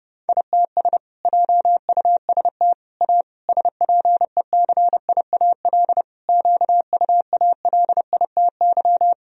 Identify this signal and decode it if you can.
Morse code, heard as a keyed tone, about 30 wpm, 700 Hz. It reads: ITS JUST A SPECIAL QUALITY
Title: Sentences from Top 1000 Words - Review Part 4 - 30wpm